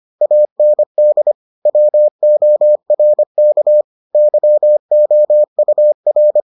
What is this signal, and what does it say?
Morse code, heard as a keyed tone, about 25 wpm, 600 Hz. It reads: AND WORK YOUR